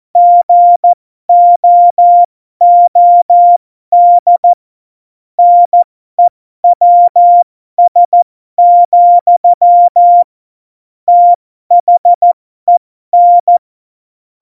Morse code, heard as a keyed tone, about 14 wpm, spaced as for 12 wpm, 700 Hz. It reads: GOOD NEWS, THEN
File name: Morsekurs / Witze